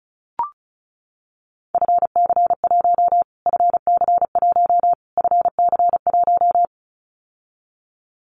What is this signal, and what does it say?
Morse code, heard as a keyed tone, about 34 wpm, 700 Hz. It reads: FC1 FC1 FC1
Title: Callsign Prefixes for Top 25 DXCC - Repeated 3x - 35wpm